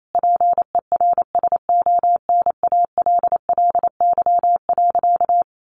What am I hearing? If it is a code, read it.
Morse code, heard as a keyed tone, about 28 wpm, 700 Hz. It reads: PERSONALLY.
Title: Sentences from Top 1400 Words 28wpm